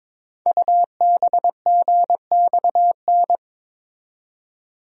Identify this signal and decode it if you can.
Morse code, heard as a keyed tone, about 22 wpm, 700 Hz. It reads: UBGXN
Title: Code Groups - 5 Letters - 22wpm